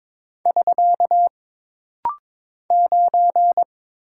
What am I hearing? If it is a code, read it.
Morse code, heard as a keyed tone, about 22 wpm, 700 Hz. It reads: <SK> 9
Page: This rapid-fire timing is geared to push your ICR (Instant Character Recognition) skills.